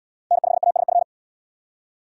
Morse code, heard as a keyed tone, about 73 wpm, 700 Hz. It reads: N5ASF